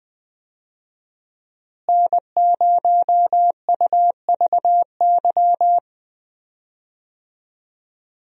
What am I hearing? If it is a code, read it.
Morse code, heard as a keyed tone, about 20 wpm, 700 Hz. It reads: N0UVY
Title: USA Call Signs - 5 Characters - Repeated 2x - 20wpm